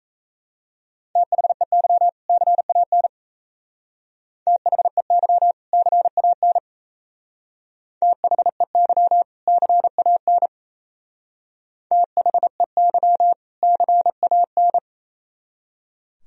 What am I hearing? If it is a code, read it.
Morse code, heard as a keyed tone, about 36 wpm, 700 Hz. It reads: THEY CAN THEY CAN THEY CAN THEY CAN